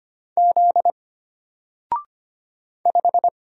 Morse code, heard as a keyed tone, about 25 wpm, 700 Hz. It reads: Z 5
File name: Single-Letter-Number-25wpm